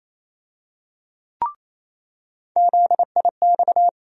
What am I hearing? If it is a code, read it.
Morse code, heard as a keyed tone, about 28 wpm, 700 Hz. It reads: ZIX
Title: USA Call Sign Postfixes - 3 Letters 28wpm